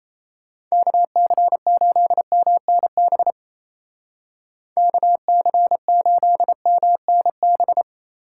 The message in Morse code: KC8MNB KC8MNB

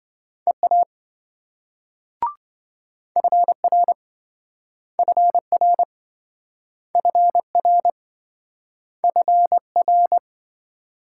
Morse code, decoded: EA FR FR FR FR

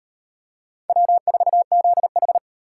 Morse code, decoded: W4ZH

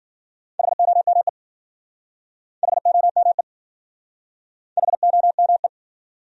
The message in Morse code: HOME HOME HOME

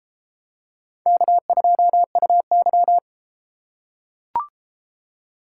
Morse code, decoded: K2UY

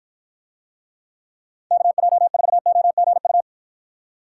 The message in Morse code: KJ4OGU

